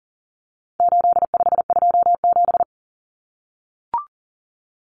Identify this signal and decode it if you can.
Morse code, heard as a keyed tone, about 40 wpm, 700 Hz. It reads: 8527
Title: Common Cut Numbers - 4-Digit 40wpm